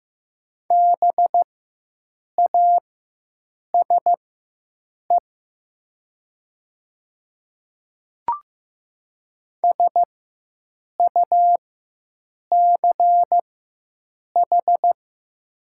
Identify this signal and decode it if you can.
Morse code, heard as a keyed tone, about 15 wpm, spaced as for 7 wpm, 700 Hz. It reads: BASE SUCH
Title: 4x Character Spacing - Top 500 Words - 15wpm